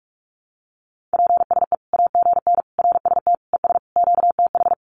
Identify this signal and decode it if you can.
Morse code, heard as a keyed tone, about 44 wpm, 700 Hz. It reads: PSE AGN RST ES QTH
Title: Common QSO Phrases 45wpm